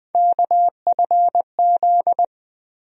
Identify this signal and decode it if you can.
Morse code, heard as a keyed tone, about 20 wpm, 700 Hz. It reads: KFZ
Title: USA Call Sign Postfixes - 3 Letters 20wpm